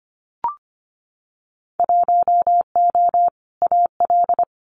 Morse code, heard as a keyed tone, about 25 wpm, 700 Hz. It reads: 1O AL
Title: Winter Field Day Exchanges 25wpm